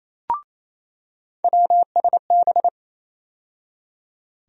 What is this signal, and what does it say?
Morse code, heard as a keyed tone, about 28 wpm, 700 Hz. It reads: WSB
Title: USA Call Sign Postfixes - 3 Letters 28wpm